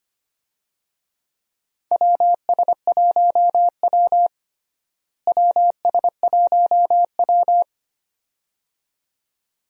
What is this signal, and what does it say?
Morse code, heard as a keyed tone, about 25 wpm, 700 Hz. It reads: WS1W WS1W
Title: USA Call Signs - 4 Characters - Repeated 2x - 25wpm